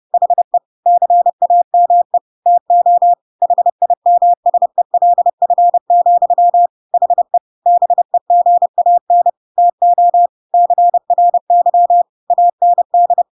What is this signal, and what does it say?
Morse code, heard as a keyed tone, about 30 wpm, 700 Hz. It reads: HE CAME TO HIMSELF, HE BEGAN TO CRY AND
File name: Pinocchio / Pinocchio ch5 30 30 ik1hge